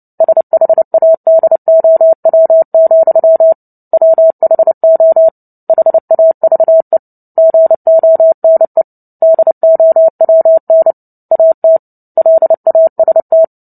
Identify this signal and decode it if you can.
Morse code, heard as a keyed tone, about 30 wpm, 650 Hz. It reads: SHADOW, WHO HAVE GONE DOWN AT LAST
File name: Book Two / 03-Chapter 3 - 29 WPM